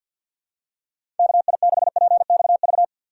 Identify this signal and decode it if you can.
Morse code, heard as a keyed tone, about 46 wpm, 700 Hz. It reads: KI6PXV
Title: USA Call Signs - 6 Characters - 3x Character Spacing - 50wpm